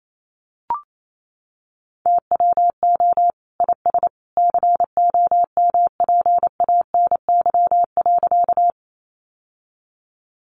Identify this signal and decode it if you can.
Morse code, heard as a keyed tone, about 28 wpm, 700 Hz. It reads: TWO IS COMPANY.
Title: Sentences from Top 600 Words 28wpm